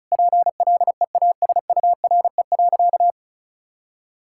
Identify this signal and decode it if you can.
Morse code, heard as a keyed tone, about 36 wpm, 700 Hz. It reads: PLEASURE.
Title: Sentences from Top 1600 Words 35wpm